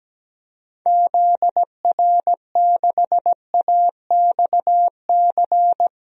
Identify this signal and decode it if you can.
Morse code, heard as a keyed tone, about 17 wpm, 700 Hz. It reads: ZR6AXC
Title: DXCC-100 Call Signs - 6 Characters - 17wpm